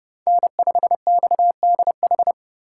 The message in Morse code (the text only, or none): N5XDH